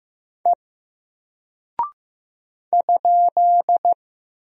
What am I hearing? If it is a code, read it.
Morse code, heard as a keyed tone, about 15 wpm, 700 Hz. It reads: E ?
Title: Single Character (Letter:Number:Punctuation:Prosign) 15wpm